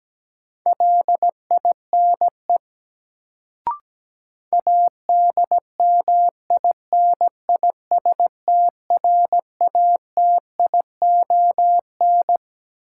Morse code, heard as a keyed tone, about 17 wpm, 700 Hz. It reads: LINE ADMINISTRATION
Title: Sets of 5 Words - New Word Drill - 17wpm